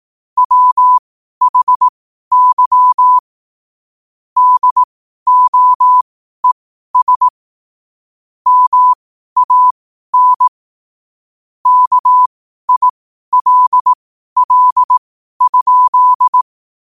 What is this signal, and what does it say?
Morse code, heard as a keyed tone, about 18 wpm, spaced as for 12 wpm, 1000 Hz. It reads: WHY DOES MAN KILL?
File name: qotdfiles / 12wpm